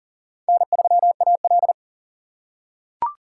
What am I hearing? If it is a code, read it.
Morse code, heard as a keyed tone, about 40 wpm, 700 Hz. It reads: N3AL
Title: USA Call Signs - 3x Character Spacing - 40wpm